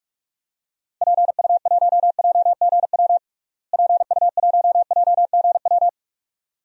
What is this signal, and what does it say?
Morse code, heard as a keyed tone, about 44 wpm, 700 Hz. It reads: PU1JGW PU1JGW
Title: DXCC-100 Call Signs - 6 Characters - Repeated 2x - 45wpm